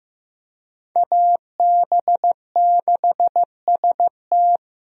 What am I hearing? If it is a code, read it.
Morse code, heard as a keyed tone, about 15 wpm, 700 Hz. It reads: AB6ST
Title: USA Call Signs - 5 Characters - Repeated 3x - 15wpm